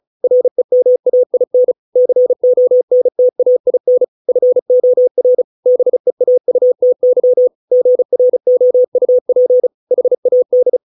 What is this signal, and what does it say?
Morse code, heard as a keyed tone, about 35 wpm, 500 Hz. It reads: REMAIN CONTAIN FOR BEAUTY GROUP HAD